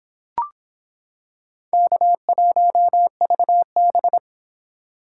Morse code, heard as a keyed tone, about 26 wpm, 700 Hz. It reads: K1VB